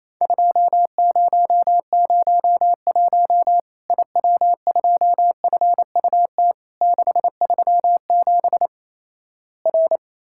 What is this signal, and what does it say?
Morse code, heard as a keyed tone, about 28 wpm, 700 Hz. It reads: 2001 IW2FUT 637 R
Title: Worked All Europe DX Contest - QTC Exchange - 28wpm